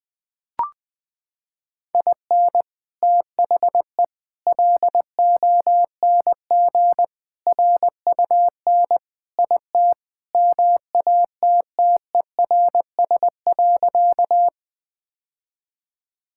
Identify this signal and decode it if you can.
Morse code, heard as a keyed tone, about 21 wpm, 700 Hz. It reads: IN THE LONG RUN IT MATTERS.
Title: Sentences with 4-Word Phrases - 20wpm